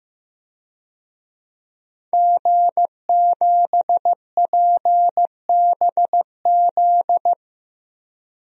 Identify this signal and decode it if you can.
Morse code, heard as a keyed tone, about 15 wpm, 700 Hz. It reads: G7PBZ
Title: DXCC-100 Call Signs - 5 Characters - Repeated 3x - 15wpm